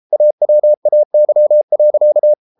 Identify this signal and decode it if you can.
Morse code, heard as a keyed tone, about 33 wpm, 600 Hz. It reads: AWAY.